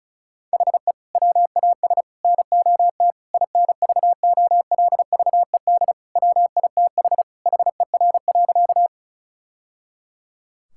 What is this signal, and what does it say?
Morse code, heard as a keyed tone, about 36 wpm, 700 Hz. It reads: HE WAS NOT INVOLVED WITH HER.
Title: Sentences from Top 1000 Words - Review Part 5 - 35wpm